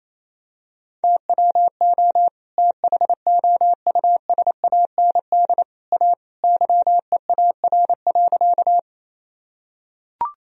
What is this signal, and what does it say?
Morse code, heard as a keyed tone, about 28 wpm, 700 Hz. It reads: TWO THOUSAND A YEAR.
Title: Sentences from Top 1000 Words - Review Part 3 - 28wpm